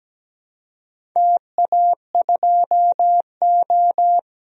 Morse code, heard as a keyed tone, about 17 wpm, 700 Hz. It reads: TA2O
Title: DXCC-100 Call Signs - 4 Characters - Repeated 2x - 17wpm